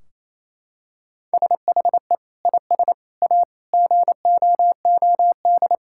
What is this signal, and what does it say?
Morse code, heard as a keyed tone, about 28 wpm, 700 Hz. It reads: SHE IS A GOOD
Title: Sentences from Top 1600 Words 28wpm